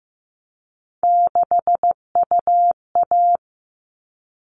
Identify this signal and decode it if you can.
Morse code, heard as a keyed tone, about 15 wpm, 700 Hz. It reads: 6UA
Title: All Cut Numbers - 3-Digit 15wpm